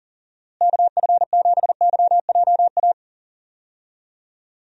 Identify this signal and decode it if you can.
Morse code, heard as a keyed tone, about 39 wpm, 700 Hz. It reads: KF7YJA
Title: USA Call Signs - 6 Characters - Repeated 3x - 40wpm